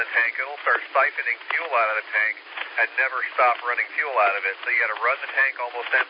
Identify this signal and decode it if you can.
Emergency Alert System (SAME) tones, heard as harsh buzzing bursts, with speech. It is heard from 1.1 to 4.9 s.